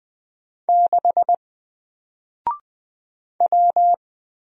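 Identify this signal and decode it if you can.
Morse code, heard as a keyed tone, about 20 wpm, 700 Hz. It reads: 6 W
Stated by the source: Single letter and number Morse drill at 20 wpm, from N4LCD.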